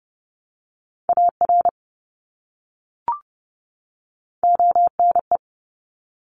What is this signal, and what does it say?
Morse code, heard as a keyed tone, about 30 wpm, 700 Hz. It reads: AR ONE
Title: ARRL Sections 30wpm